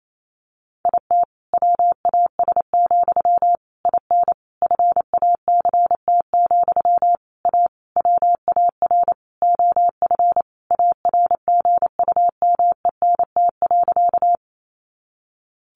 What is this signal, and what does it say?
Morse code, heard as a keyed tone, about 28 wpm, 700 Hz. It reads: IT WAS, IN FACT, A WAR OF ARGUMENT.
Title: Sentences from Top 1400 Words 28wpm